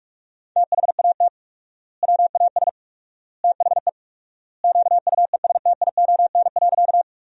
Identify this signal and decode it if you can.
Morse code, heard as a keyed tone, about 46 wpm, 700 Hz. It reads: THAT WAS THE QUESTION.
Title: 4x Word-Spacing - Sentences from Top 400 Words - 45wpm